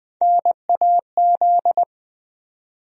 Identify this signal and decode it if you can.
Morse code, heard as a keyed tone, about 20 wpm, 700 Hz. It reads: NAZ
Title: USA Call Sign Postfixes - 3 Letters 20wpm